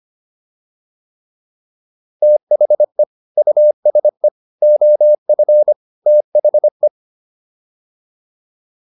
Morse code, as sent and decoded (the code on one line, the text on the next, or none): - .... / ..- ... / --- ..-. / - .... .
THE USE OF THE